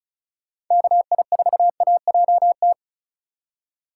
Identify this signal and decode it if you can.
Morse code, heard as a keyed tone, about 45 wpm, 700 Hz. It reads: KI4AJT